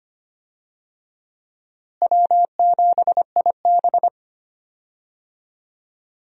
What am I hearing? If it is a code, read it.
Morse code, heard as a keyed tone, about 25 wpm, 700 Hz. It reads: W7IB
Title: USA Call Signs - 4 Characters - Repeated 2x - 25wpm